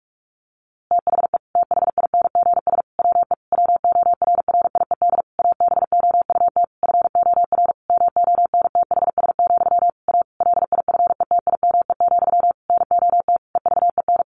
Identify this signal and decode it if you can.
Morse code, heard as a keyed tone, about 44 wpm, 700 Hz. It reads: THE THINGS WE WORRIED ABOUT FOR MONTHS, A LIFETIME, NOT EVEN
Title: A Sound Of Thunder 45wpm